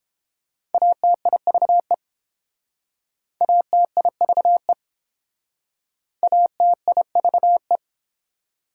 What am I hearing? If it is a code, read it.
Morse code, heard as a keyed tone, about 27 wpm, 700 Hz. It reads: ATIVE ATIVE ATIVE
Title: Speed-Racing Prefixes and Suffixes 22wpm